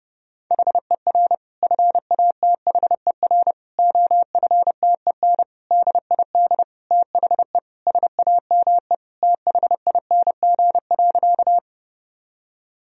Morse code, heard as a keyed tone, about 30 wpm, 700 Hz. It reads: HER FATHER OFTEN DID THE SAME THING.
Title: Sentences from Top 1000 Words - Review Part 1 - 30wpm